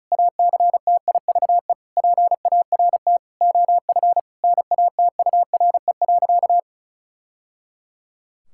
Morse code, heard as a keyed tone, about 36 wpm, 700 Hz. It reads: ACTIVE PART OF NATURE.
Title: Sentences from Top 1000 Words - Review Part 6 - 35wpm